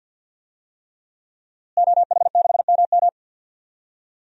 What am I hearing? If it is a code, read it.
Morse code, heard as a keyed tone, about 51 wpm, 700 Hz. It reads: OH6MM
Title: DXCC-100 Call Signs - 5 Characters - Repeated 3x - 50wpm